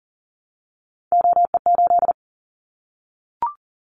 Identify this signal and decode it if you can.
Morse code, heard as a keyed tone, about 41 wpm, 700 Hz. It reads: OE8
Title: Callsign Prefixes for Top 100 DXCC - 40wpm